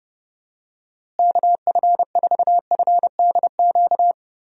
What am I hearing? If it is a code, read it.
Morse code, heard as a keyed tone, about 29 wpm, 700 Hz. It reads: KF4FDQ
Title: USA Call Signs - 6 Characters - Repeated 3x - 30wpm